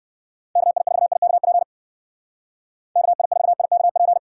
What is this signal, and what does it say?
Morse code, heard as a keyed tone, about 59 wpm, 700 Hz. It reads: KI4IKP KI4IKP